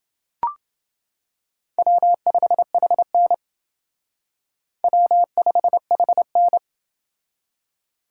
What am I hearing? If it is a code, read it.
Morse code, heard as a keyed tone, about 29 wpm, 700 Hz. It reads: W5HN W5HN